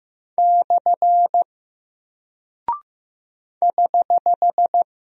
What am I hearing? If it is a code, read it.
Morse code, heard as a keyed tone, about 15 wpm, 700 Hz. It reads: / <HH>
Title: Single Character (Letter:Number:Punctuation:Prosign) Alt 15wpm